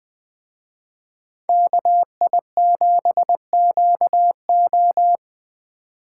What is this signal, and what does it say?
Morse code, heard as a keyed tone, about 20 wpm, 700 Hz. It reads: KI7QO